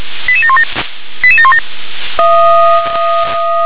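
Five-tone selective call (ZVEI), heard as quick runs of five tones, tones of about 70 ms. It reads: 88717 88717
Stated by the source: Five-tone ZVEI selective call received by SDR.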